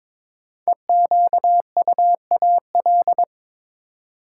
Morse code, heard as a keyed tone, about 22 wpm, 700 Hz. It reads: EQUAL
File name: 5 Letter Words - First 3 Letters Given - 22wpm